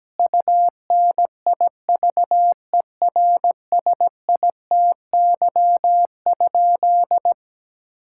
Morse code, heard as a keyed tone, about 17 wpm, 700 Hz. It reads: UNIVERSITY?
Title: Sentences from Top 1000 Words - Review Part 5 - 17wpm